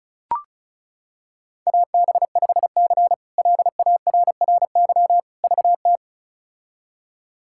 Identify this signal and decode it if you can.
Morse code, heard as a keyed tone, about 34 wpm, 700 Hz. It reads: AB5C LARRY VT
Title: K1USN SST QSO Practice - 35wpm